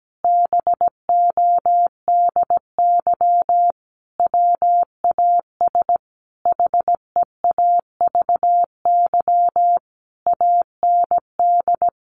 Morse code, heard as a keyed tone, about 17 wpm, 700 Hz. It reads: BODY WAS HEAVY AND
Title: Sentences from Top 1400 Words 17wpm